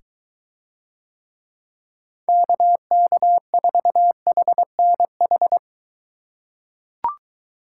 Morse code, heard as a keyed tone, about 22 wpm, 700 Hz. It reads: KK4HNH